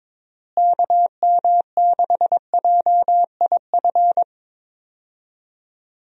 Morse code, heard as a keyed tone, about 22 wpm, 700 Hz. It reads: KM6JIF
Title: USA Call Signs - 6 Characters - Repeated 2x - 22wpm